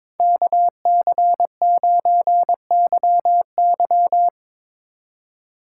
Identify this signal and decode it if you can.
Morse code, heard as a keyed tone, about 22 wpm, 700 Hz. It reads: KC9YY
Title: Speed Racing - USA 2x2 Call Signs 22wpm